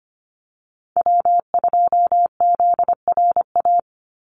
Morse code, heard as a keyed tone, about 25 wpm, 700 Hz. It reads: W2ZRA
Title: Call Signs Repeated 2x - 25wpm